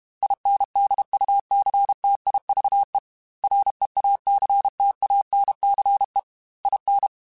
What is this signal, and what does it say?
Morse code, heard as a keyed tone, about 32 wpm, 800 Hz. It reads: INDUCTIVE REACTANCE IN
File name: Technical Subjects / Technical Subjects-mp3s / XL @30